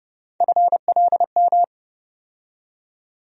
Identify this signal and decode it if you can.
Morse code, heard as a keyed tone, about 30 wpm, 700 Hz. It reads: FLM
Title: USA Call Sign Postfixes - 3 Letters 30wpm